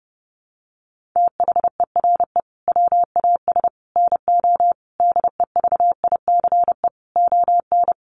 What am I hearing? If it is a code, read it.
Morse code, heard as a keyed tone, about 30 wpm, 700 Hz. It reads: THERE WAS NO DEVICE ON
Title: Sentences from Top 1500 Words 30wpm